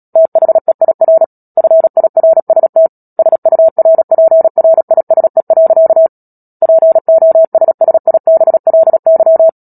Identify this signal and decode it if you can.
Morse code, heard as a keyed tone, about 37 wpm, 650 Hz. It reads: THEIR FIRST SURPRISE. POSSIBLY
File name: Book Two / 10-Chapter 10 - Epilogue - 36 WPM